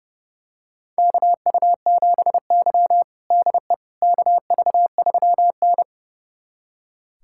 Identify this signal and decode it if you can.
Morse code, heard as a keyed tone, about 30 wpm, 700 Hz. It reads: KU7Y DE KV3N
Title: USA Callsign DE Callsign - 4 Characters - 30wpm